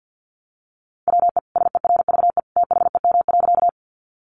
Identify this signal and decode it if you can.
Morse code, heard as a keyed tone, about 50 wpm, 700 Hz. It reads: WE SERVE THEM.
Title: Sentences from Top 400 Words 50wpm